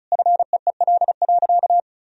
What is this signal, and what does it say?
Morse code, heard as a keyed tone, about 34 wpm, 700 Hz. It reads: FEEL.
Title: Sentences from Top 1000 Words - Review Part 1 - 35wpm